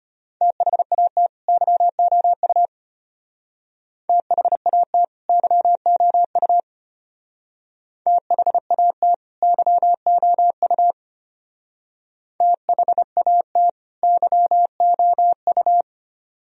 Morse code, heard as a keyed tone, about 33 wpm, 700 Hz. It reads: THAT YOU THAT YOU THAT YOU THAT YOU